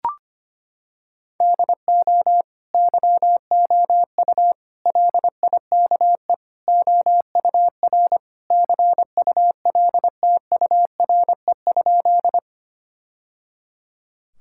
Morse code, heard as a keyed tone, about 25 wpm, 700 Hz. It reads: DO YOU LIKE OUR CULTURE?
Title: Sentences from Top 1000 Words - Review Part 6 - 25wpm